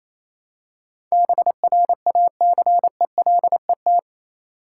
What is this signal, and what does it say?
Morse code, heard as a keyed tone, about 27 wpm, 700 Hz. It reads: BRACELET
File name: Easy Words - Speed-Racing with Hints - 22wpm